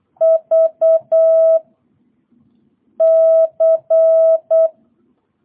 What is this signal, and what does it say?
Morse code, heard as a keyed tone, about 8 wpm, 660 Hz. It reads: V C